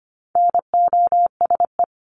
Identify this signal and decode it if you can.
Morse code, heard as a keyed tone, about 25 wpm, 700 Hz. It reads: NOSE
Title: Top 1501-1750 Words 25wpm